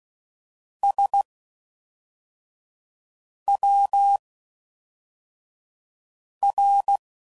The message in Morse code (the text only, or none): SWR